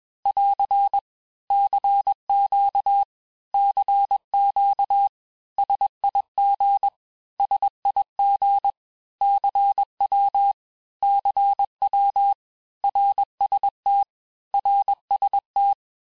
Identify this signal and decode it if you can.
Morse code, heard as a keyed tone, about 21 wpm, 800 Hz. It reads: <AR> CQ CQ SIG SIG CW CW RST RST
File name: cours-du-lundi files / cours-1592--teste-abreviations-vitesse-15mpm-puis-20mpm